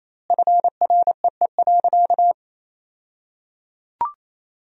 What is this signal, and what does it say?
Morse code, heard as a keyed tone, about 28 wpm, 700 Hz. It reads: FREE.
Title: 2x Word-Spacing - Sentences from Top 500 Words - 28wpm